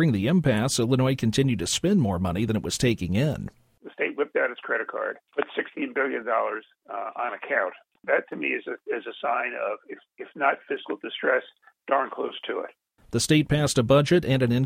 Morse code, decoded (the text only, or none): none